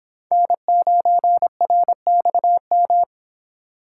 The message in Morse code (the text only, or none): N9RXM